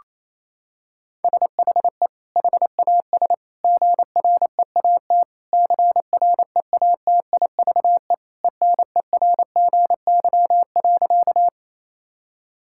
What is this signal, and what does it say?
Morse code, heard as a keyed tone, about 28 wpm, 700 Hz. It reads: SHE HAS GREAT CREATIVE ENERGY.
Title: Sentences from Top 1000 Words - Review Part 6 - 28wpm